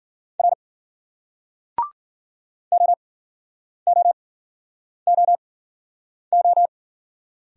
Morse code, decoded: R OOOO

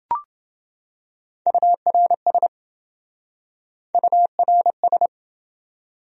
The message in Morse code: URS URS

